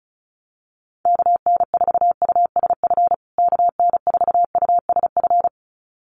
Morse code, heard as a keyed tone, about 34 wpm, 700 Hz. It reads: KN4USF KN4USF
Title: USA Call Signs Repeated 2x 35wpm